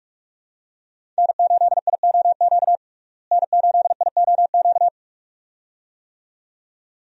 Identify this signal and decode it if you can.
Morse code, heard as a keyed tone, about 47 wpm, 700 Hz. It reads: N8IOQ N8IOQ
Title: USA Call Signs - 5 Characters - Repeated 2x - 45wpm